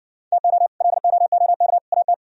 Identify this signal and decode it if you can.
Morse code, heard as a keyed tone, about 60 wpm, 700 Hz. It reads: TO LOOK AT